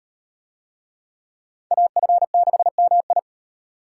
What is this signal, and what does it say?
Morse code, heard as a keyed tone, about 38 wpm, 700 Hz. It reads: AF6MI